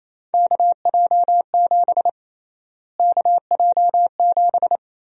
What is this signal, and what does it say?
Morse code, heard as a keyed tone, about 28 wpm, 700 Hz. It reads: KJ7 KJ7
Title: USA Call Sign Prefixes - Repeated 2x - 28wpm